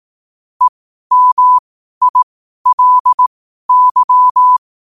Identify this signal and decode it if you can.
Morse code, heard as a keyed tone, about 18 wpm, spaced as for 12 wpm, 1000 Hz. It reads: EMILY
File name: qotdfiles / 12wpm